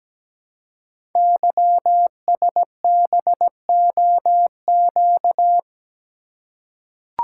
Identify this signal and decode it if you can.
Morse code, heard as a keyed tone, about 17 wpm, 700 Hz. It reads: YSBOQ
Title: Code Groups - 5 Letters - 17wpm